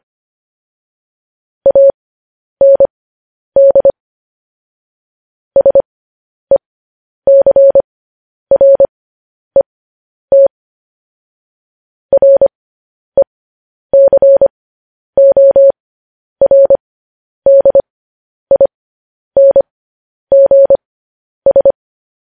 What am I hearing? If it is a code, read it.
Morse code, heard as a keyed tone, about 25 wpm, spaced as for 10 wpm, 550 Hz. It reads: AND SECRET RECORDINGS
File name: podcast / News headlines-1x-25x10